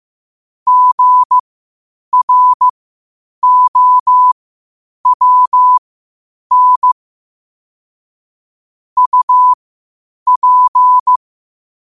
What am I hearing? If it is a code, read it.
Morse code, heard as a keyed tone, about 15 wpm, spaced as for 8 wpm, 1000 Hz. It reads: GROWN UP
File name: qotdfiles / 8wpm